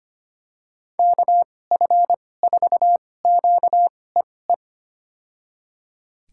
Morse code, heard as a keyed tone, about 25 wpm, spaced as for 18 wpm, 700 Hz. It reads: KF4QEE
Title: USA Call Signs - 6 Characters - 2x Character Spacing - 25wpm